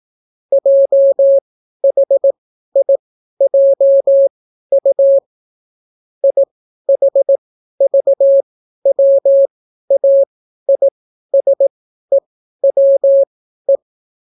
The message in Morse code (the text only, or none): JHIJU IHVWAISEWE